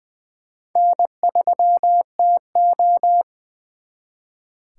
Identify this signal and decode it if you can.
Morse code, heard as a keyed tone, about 23 wpm, 700 Hz. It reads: N3TO
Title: Speed Racing - USA 1x2 Call Signs 20wpm